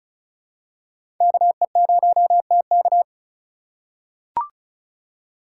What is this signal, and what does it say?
Morse code, heard as a keyed tone, about 36 wpm, 700 Hz. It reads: KE0TK
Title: USA Call Signs - 5 Characters - 2x Character Spacing - 35wpm